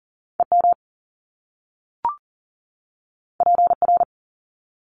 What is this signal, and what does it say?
Morse code, heard as a keyed tone, about 40 wpm, 700 Hz. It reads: EM PR
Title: Top Two Letter Combinations in Words 40wpm